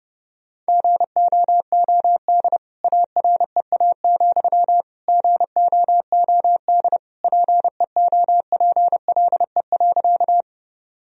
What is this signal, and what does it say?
Morse code, heard as a keyed tone, about 30 wpm, 700 Hz. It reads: GOOD AREA, GOOD PEOPLE.
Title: Sentences from Top 1000 Words - Review Part 2 - 30wpm